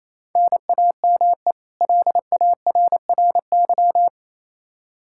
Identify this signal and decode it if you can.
Morse code, heard as a keyed tone, about 27 wpm, 700 Hz. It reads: NAME LARRY
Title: Practice QSO - Exchange 1 - Beginning - 28wpm